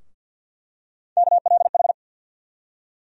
Morse code, heard as a keyed tone, about 47 wpm, 690 Hz. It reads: KLH
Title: USA Call Sign Postfixes - 3 Letters 50wpm